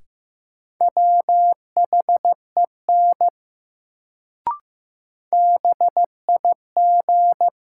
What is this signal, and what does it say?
Morse code, heard as a keyed tone, about 15 wpm, 700 Hz. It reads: WHEN BIG
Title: Sets of 2 Words - New Word Drill - 15wpm